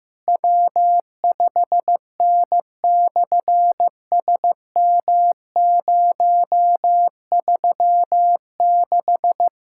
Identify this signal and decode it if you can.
Morse code, heard as a keyed tone, about 15 wpm, 700 Hz. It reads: W5N/SM036
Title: SOTA QSO Practice 15wpm